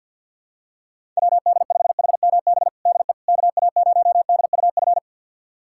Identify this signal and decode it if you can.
Morse code, heard as a keyed tone, about 47 wpm, 700 Hz. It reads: WD5HMB DE KA0DUF
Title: USA Callsign DE Callsign 50wpm